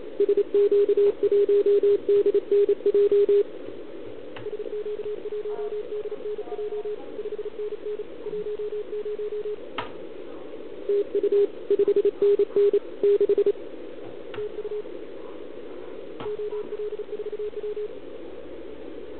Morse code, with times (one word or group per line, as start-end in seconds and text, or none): 0.2-3.4 SQ1DNJ
4.5-7.0 SQ1DNJ
7.2-8.0 5NN
8.3-9.5 10
10.9-11.5 TU
11.7-12.8 5NN
13.0-13.5 6
14.4-14.8 TU
16.3-17.9 OL4W